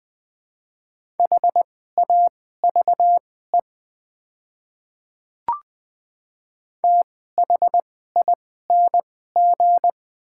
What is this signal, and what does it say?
Morse code, heard as a keyed tone, about 20 wpm, spaced as for 14 wpm, 700 Hz. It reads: HAVE THING
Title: 2x Character Spacing - Top 100 Words - 20wpm